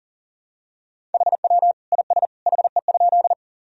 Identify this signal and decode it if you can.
Morse code, heard as a keyed tone, about 40 wpm, 700 Hz. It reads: HW IS HE?
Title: Sentences from Top 100 Words and Abbreviations 40wpm